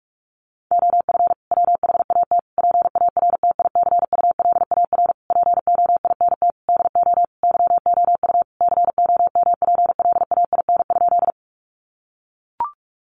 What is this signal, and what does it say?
Morse code, heard as a keyed tone, about 44 wpm, 700 Hz. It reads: OF WHAT PARTICULAR POINT DO YOU COMPLAIN?
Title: Sentences from Top 1300 Words 45wpm